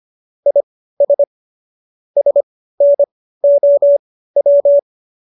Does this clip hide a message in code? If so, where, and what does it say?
Yes: Morse code from 0.5 to 4.8 s, heard as a keyed tone, about 25 wpm, spaced as for 15 wpm, 570 Hz. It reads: IS SNOW